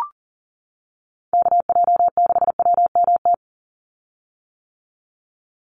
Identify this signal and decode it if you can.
Morse code, heard as a keyed tone, about 40 wpm, 700 Hz. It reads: KJ6WMT
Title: USA Call Signs 40wpm